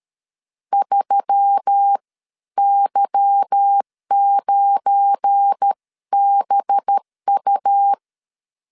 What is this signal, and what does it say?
Morse code, heard as a keyed tone, about 13 wpm, spaced as for 11 wpm, 800 Hz. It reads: 3Y9BU
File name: CW / DET / 1 9 1500C S C 12wpm 3811